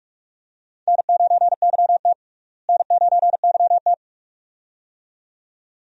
Morse code, heard as a keyed tone, about 45 wpm, 700 Hz. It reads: N9YT N9YT
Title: USA Call Signs - 4 Characters - Repeated 2x - 45wpm